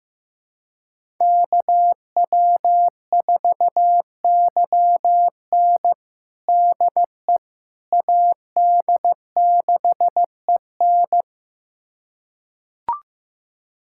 Morse code, heard as a keyed tone, about 16 wpm, 700 Hz. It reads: KW4YN DE AD6EN
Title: USA Callsign DE Callsign - 5 Characters - 15wpm